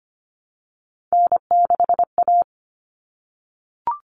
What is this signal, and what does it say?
Morse code, heard as a keyed tone, about 25 wpm, 700 Hz. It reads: N6A
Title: All Cut Numbers - 3-Digit 25wpm